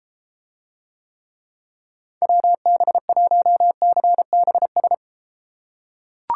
Morse code, heard as a keyed tone, about 33 wpm, 700 Hz. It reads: WB1CBS